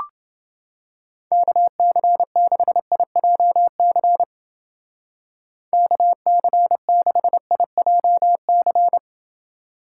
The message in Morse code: KC6IJC KC6IJC

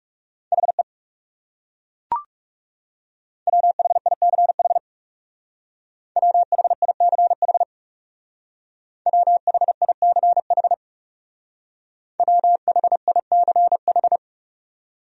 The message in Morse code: HE WHICH WHICH WHICH WHICH

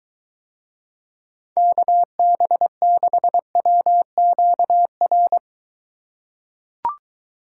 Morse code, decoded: KB6WQR

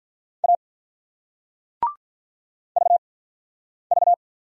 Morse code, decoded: A V V

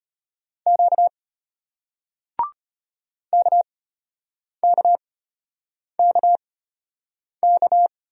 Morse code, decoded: Q KKKK

